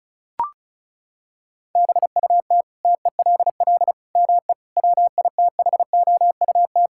Morse code, heard as a keyed tone, about 36 wpm, 700 Hz. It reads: BUT TELL ME WITHOUT
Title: Sentences from Top 1000 Words - Review Part 6 - 35wpm